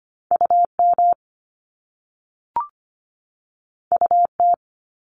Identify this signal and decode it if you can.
Morse code, heard as a keyed tone, about 25 wpm, 700 Hz. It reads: UM UT
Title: Top Two Letter Combinations in Words 25wpm